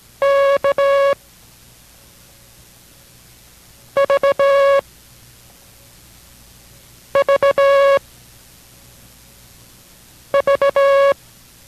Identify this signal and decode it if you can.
Morse code, heard as a keyed tone, about 17 wpm, 560 Hz. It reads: K V V V